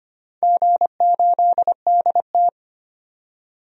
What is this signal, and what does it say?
Morse code, heard as a keyed tone, about 25 wpm, 700 Hz. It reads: G8DT
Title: DXCC-100 Call Signs - 4 Characters - 25wpm